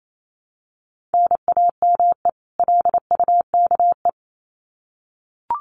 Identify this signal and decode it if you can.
Morse code, heard as a keyed tone, about 28 wpm, 700 Hz. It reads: NAME LUKE
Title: Common QSO Phrases 28wpm